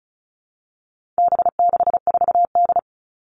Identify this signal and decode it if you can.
Morse code, heard as a keyed tone, about 34 wpm, 700 Hz. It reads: B64D
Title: All Cut Numbers - 4-Digit 35wpm